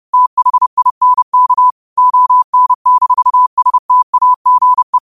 Morse code, heard as a keyed tone, about 30 wpm, 1000 Hz. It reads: THINK ON-STAGE
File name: qotdfiles / 30wpm